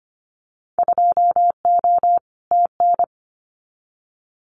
Morse code, heard as a keyed tone, about 25 wpm, 700 Hz. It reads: 2O TN
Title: Winter Field Day Exchanges 25wpm